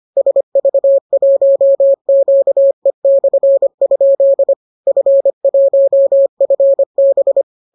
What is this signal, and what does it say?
Morse code, heard as a keyed tone, about 25 wpm, 550 Hz. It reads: SV1QE/? F1FB